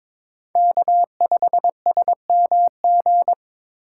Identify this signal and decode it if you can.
Morse code, heard as a keyed tone, about 23 wpm, 700 Hz. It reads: K5SMG
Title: USA Call Signs - 3x Character Spacing - 22wpm